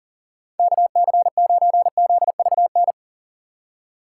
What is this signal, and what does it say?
Morse code, heard as a keyed tone, about 37 wpm, 700 Hz. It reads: KC9ZVN
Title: USA Call Signs - 6 Characters - 6x Character Spacing - 40wpm